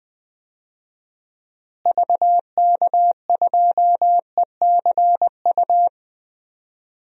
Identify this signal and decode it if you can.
Morse code, heard as a keyed tone, about 21 wpm, 700 Hz. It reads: VK2ECU
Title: DXCC-100 Call Signs - 6 Characters - Repeated 3x - 20wpm